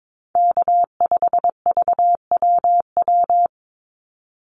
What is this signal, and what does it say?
Morse code, heard as a keyed tone, about 22 wpm, 700 Hz. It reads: K5VWW
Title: Call Signs Repeated 2x - 22wpm